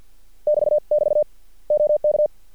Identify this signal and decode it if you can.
Morse code, heard as a keyed tone, about 50 wpm, spaced as for 43 wpm, 600 Hz. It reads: <BT><BT> OK
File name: archived / CRSnet